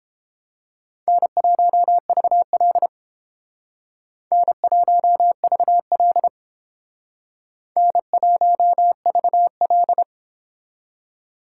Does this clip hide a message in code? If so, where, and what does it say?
Yes: Morse code from 1.1 to 10.0 s, heard as a keyed tone, about 31 wpm, 700 Hz. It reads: N1VL N1VL N1VL